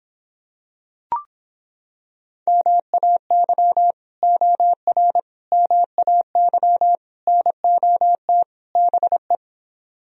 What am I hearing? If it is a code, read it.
Morse code, heard as a keyed tone, about 26 wpm, 700 Hz. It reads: MAY OR MAY NOT BE